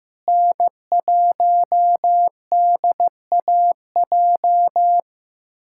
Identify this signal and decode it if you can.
Morse code, heard as a keyed tone, about 15 wpm, 700 Hz. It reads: N1DAJ
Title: USA Callsign DE Callsign 15wpm